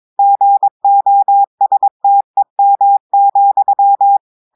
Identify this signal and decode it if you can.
Morse code, heard as a keyed tone, about 22 wpm, 800 Hz. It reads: GOSTEM,
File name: audios / portugais / portugais texte en clair 2 vitesse 22 mots minutes